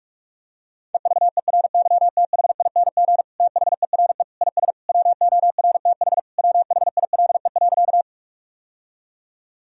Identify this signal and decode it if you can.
Morse code, heard as a keyed tone, about 41 wpm, 700 Hz. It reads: EVERYTHING THERE IS WORTH WHILE.
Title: Sentences from Top 1000 Words - Review Part 1 - 45wpm